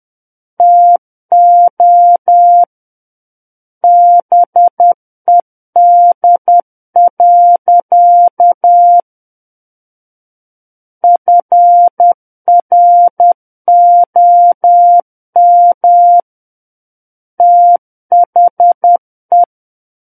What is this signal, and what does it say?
Morse code, heard as a keyed tone, about 10 wpm, 700 Hz. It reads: TO BED. FROM THE
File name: Book One / 01-Chapter 1 - 10 WPM